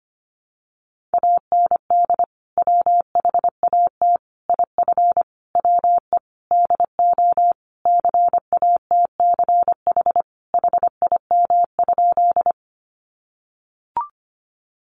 Morse code, heard as a keyed tone, about 25 wpm, 700 Hz. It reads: AND WHAT IF WE DO CATCH HIM?
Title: Sentences from Top 1500 Words 25wpm